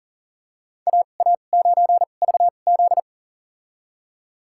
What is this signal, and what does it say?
Morse code, heard as a keyed tone, about 40 wpm, spaced as for 29 wpm, 700 Hz. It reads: AA9VZ
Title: USA Call Signs - 2x Character Spacing - 40wpm